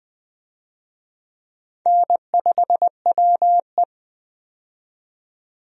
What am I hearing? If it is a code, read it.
Morse code, heard as a keyed tone, about 20 wpm, 700 Hz. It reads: N5WE
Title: USA Call Signs - 4 Characters - 20wpm